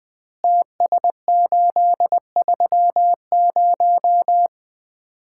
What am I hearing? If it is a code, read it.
Morse code, heard as a keyed tone, about 20 wpm, 700 Hz. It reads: TS830
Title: Common QSO Phrases - 4x Word-Spacing - 20wpm